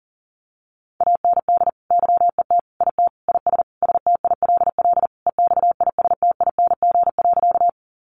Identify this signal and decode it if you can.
Morse code, heard as a keyed tone, about 40 wpm, 700 Hz. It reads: AND YET IT IS STILL EXISTING.
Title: Sentences from Top 1500 Words 40wpm